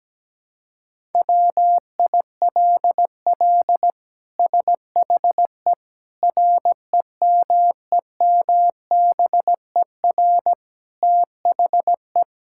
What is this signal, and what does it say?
Morse code, heard as a keyed tone, about 17 wpm, 700 Hz. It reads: WILL SHE REMEMBER THE
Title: Sentences from Top 1000 Words - Review Part 6 - 17wpm